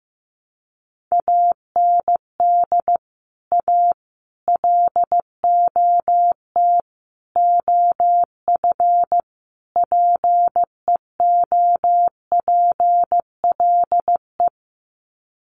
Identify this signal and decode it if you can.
Morse code, heard as a keyed tone, about 15 wpm, 700 Hz. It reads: AND A LOT OF PEOPLE
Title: Sets of 5 Words 15wpm